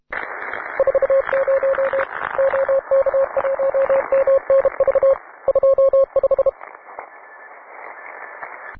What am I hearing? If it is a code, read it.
Morse code, heard as a keyed tone, about 32 wpm, 540 Hz, with radio noise. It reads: 40 OK1MNV 25